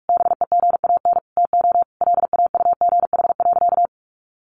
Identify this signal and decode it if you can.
Morse code, heard as a keyed tone, about 44 wpm, 700 Hz. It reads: BEGAN TO LAUGH.
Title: Sentences from Top 600 Words 45wpm